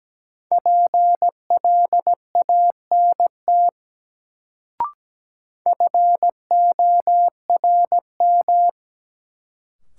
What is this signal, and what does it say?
Morse code, heard as a keyed tone, about 17 wpm, 700 Hz. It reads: PLANT FORM
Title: Sentences from Top 1000 Words - Review Part 1 - New Word Drill - 17wpm